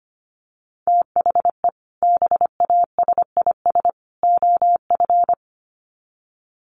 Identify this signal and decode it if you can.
Morse code, heard as a keyed tone, about 25 wpm, 700 Hz. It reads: THE BASIS OF
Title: Sets of 3 Words 25wpm